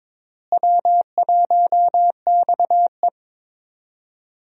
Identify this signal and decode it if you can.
Morse code, heard as a keyed tone, about 21 wpm, 700 Hz. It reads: W1XE
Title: USA Call Signs - 4 Characters - Repeated 3x - 22wpm